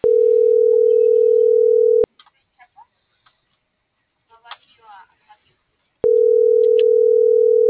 Ringback tone (North America), 440 + 480 Hz.